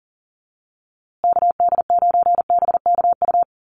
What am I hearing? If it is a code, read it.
Morse code, heard as a keyed tone, about 40 wpm, 700 Hz. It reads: KD9BKU
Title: USA Call Signs Repeated 2x 40wpm